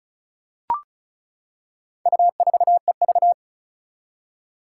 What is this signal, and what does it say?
Morse code, heard as a keyed tone, about 36 wpm, 700 Hz. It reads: U4EV
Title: Four Challenging Letter-Number Combinations 35wpm